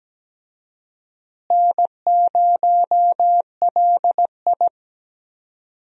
Morse code, heard as a keyed tone, about 17 wpm, 700 Hz. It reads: N0LI
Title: USA Call Signs - 4 Characters - Repeated 3x - 17wpm